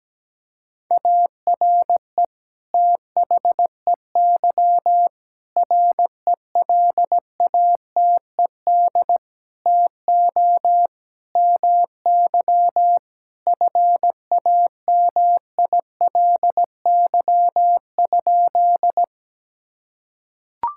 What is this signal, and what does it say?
Morse code, heard as a keyed tone, about 17 wpm, 700 Hz. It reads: ARE THEY RELATED TO MY FAMILY?
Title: Sentences from Top 1000 Words - Review Part 5 - 17wpm